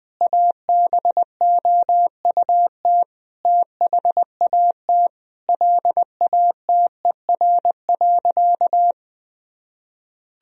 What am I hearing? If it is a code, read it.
Morse code, heard as a keyed tone, about 20 wpm, 700 Hz. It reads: ABOUT THAT LATER.
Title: Sentences from Top 1000 Words - Review Part 4 - 20wpm